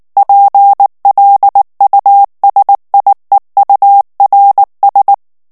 Morse code, heard as a keyed tone, about 19 wpm, 800 Hz. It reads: PLUSIEURS
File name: morse mp3 / FRANCAIS TEXTE CLAIR 1 VITESSE 18 MOTS MINUTES